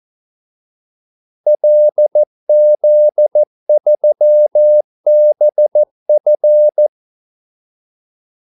Morse code, heard as a keyed tone, about 14 wpm, 600 Hz. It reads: LZ3BF
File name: Callsigns-14wpm